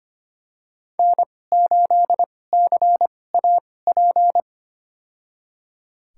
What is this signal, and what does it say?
Morse code, heard as a keyed tone, about 25 wpm, spaced as for 18 wpm, 700 Hz. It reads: N8CAP
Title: USA Call Signs - 5 Characters - 2x Character Spacing - 25wpm